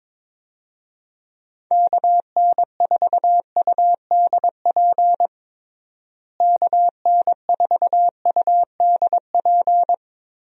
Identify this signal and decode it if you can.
Morse code, heard as a keyed tone, about 22 wpm, 700 Hz. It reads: KN4UDP KN4UDP
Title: USA Call Signs - 6 Characters - Repeated 3x - 22wpm